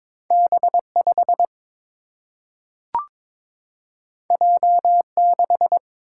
Morse code, heard as a keyed tone, about 22 wpm, 700 Hz. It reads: B 5 J 6
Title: Single-Letter Single-Number 22wpm